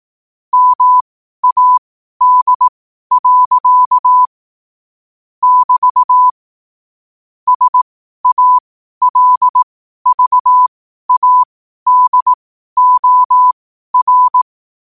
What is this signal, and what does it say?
Morse code, heard as a keyed tone, about 18 wpm, spaced as for 12 wpm, 1000 Hz. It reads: MAD. <BT> SALVADOR
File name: qotdfiles / 12wpm